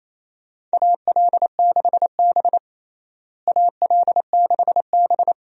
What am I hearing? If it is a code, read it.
Morse code, heard as a keyed tone, about 28 wpm, 700 Hz. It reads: AL6B AL6B
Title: USA Call Signs - 4 Characters - Repeated 3x - 28wpm